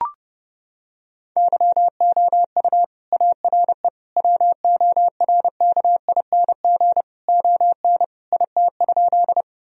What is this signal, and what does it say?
Morse code, heard as a keyed tone, about 30 wpm, 700 Hz. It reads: YOU ARE WORKING ON IT?
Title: Sentences from Top 1000 Words - Review Part 4 - 30wpm